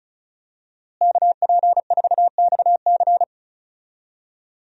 Morse code, heard as a keyed tone, about 36 wpm, 700 Hz. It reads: KP4XC
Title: DXCC-100 Call Signs - 5 Characters - Repeated 3x - 35wpm